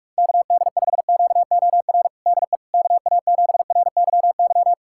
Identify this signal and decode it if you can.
Morse code, heard as a keyed tone, about 45 wpm, 690 Hz. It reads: KD5QOR DE KA7RYY
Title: USA Callsign DE Callsign - 6 Characters - 45wpm